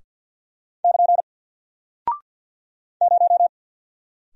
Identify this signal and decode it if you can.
Morse code, heard as a keyed tone, about 48 wpm, 700 Hz. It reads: <KN> 0
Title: Single Character (Letter:Number:Punctuation:Prosign) Alt 50wpm